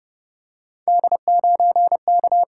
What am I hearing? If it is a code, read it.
Morse code, heard as a keyed tone, about 32 wpm, 700 Hz. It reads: D9K